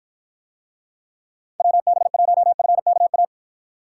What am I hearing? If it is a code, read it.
Morse code, heard as a keyed tone, about 52 wpm, 700 Hz. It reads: WB1FKA